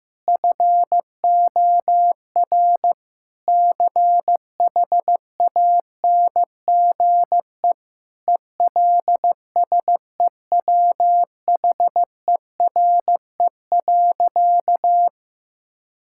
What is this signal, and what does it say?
Morse code, heard as a keyed tone, about 15 wpm, 700 Hz. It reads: FOR CHANGE ELSEWHERE.
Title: Sentences from Top 1600 Words 15wpm